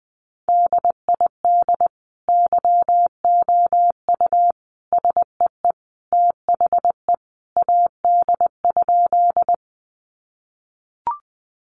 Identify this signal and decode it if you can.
Morse code, heard as a keyed tone, about 20 wpm, 700 Hz. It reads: DID YOU SEE THE AD?
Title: Sentences from Top 1300 Words 20wpm